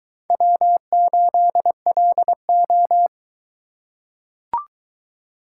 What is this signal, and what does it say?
Morse code, heard as a keyed tone, about 22 wpm, 700 Hz. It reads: W8LO